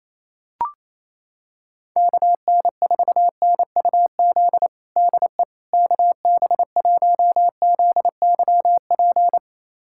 Morse code, heard as a keyed tone, about 28 wpm, 700 Hz. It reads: KN4NUZ DE KB1ZYP
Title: USA Callsign DE Callsign 28wpm